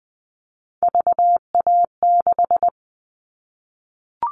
Morse code, heard as a keyed tone, about 20 wpm, 700 Hz. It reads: VA6
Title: Callsign Prefixes for Top 100 DXCC - 20wpm